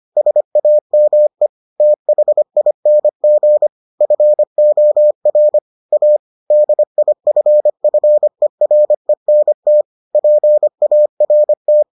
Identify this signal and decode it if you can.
Morse code, heard as a keyed tone, about 25 wpm, 600 Hz. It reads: SAME THING FOR A DIFFERENT PART